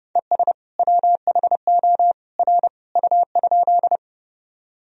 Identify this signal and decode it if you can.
Morse code, heard as a keyed tone, about 30 wpm, 700 Hz. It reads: ES WHO R U?
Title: Sentences from Top 100 Words and Abbreviations 30wpm